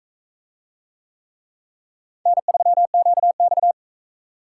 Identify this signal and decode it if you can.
Morse code, heard as a keyed tone, about 40 wpm, 700 Hz. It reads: N3QX